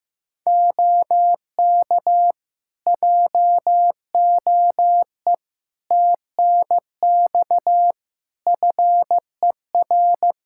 Morse code, heard as a keyed tone, about 15 wpm, 700 Hz. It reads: OK JOE TNX FER
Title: K1USN SST QSO Practice 15wpm